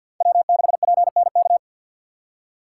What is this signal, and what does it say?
Morse code, heard as a keyed tone, about 50 wpm, 700 Hz. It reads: W6PNK
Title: USA Call Signs - 3x Character Spacing - 50wpm